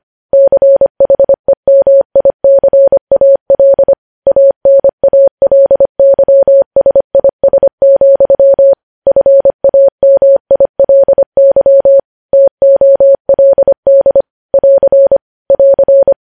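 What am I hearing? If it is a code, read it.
Morse code, heard as a keyed tone, about 25 wpm, 550 Hz. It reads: CHEMICAL ANALYSIS, FAMILY TOLD <AR> <AR>
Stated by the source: Morse practice sending news headlines at 25 wpm.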